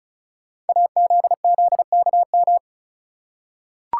Morse code, heard as a keyed tone, about 36 wpm, 690 Hz. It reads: AZZKM
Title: Code Groups - 5 Letters - 35wpm